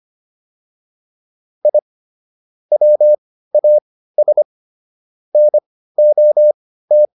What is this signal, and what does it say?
Morse code, heard as a keyed tone, about 25 wpm, spaced as for 15 wpm, 600 Hz. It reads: I WAS NOT